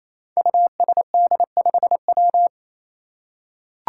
Morse code, heard as a keyed tone, about 28 wpm, 700 Hz. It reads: USD5W
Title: Five Challenging Letter-Number Combinations 28wpm